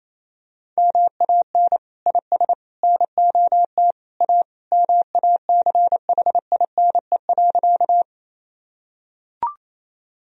Morse code, heard as a keyed tone, about 28 wpm, 700 Hz. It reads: MAN IS NOT A MACHINE.
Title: Sentences from Top 1000 Words - Review Part 2 - 28wpm